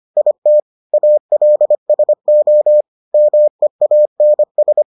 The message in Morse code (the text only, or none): IT ALSO MEANS